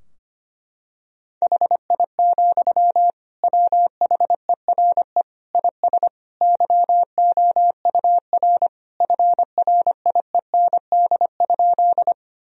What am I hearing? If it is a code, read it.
Morse code, heard as a keyed tone, about 25 wpm, 700 Hz. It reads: HI, WHERE IS YOUR FRIEND?
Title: Sentences from Top 1000 Words - Review Part 3 - 25wpm